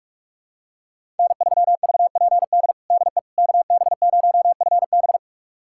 Morse code, heard as a keyed tone, about 46 wpm, 700 Hz. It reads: N3VPD DE KB0FB
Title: USA Callsign DE Callsign - 5 Characters - 45wpm